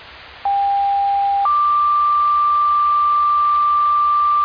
Two-tone page, heard as a one-second tone, then a three-second tone, about 800 then 1180 Hz. It is heard from 0.4 to 4.5 s.